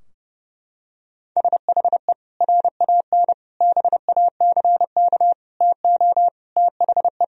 Morse code, heard as a keyed tone, about 32 wpm, 700 Hz. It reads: SHE RAN BACK TO THE
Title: Sentences from Top 1000 Words - Review Part 3 - 30wpm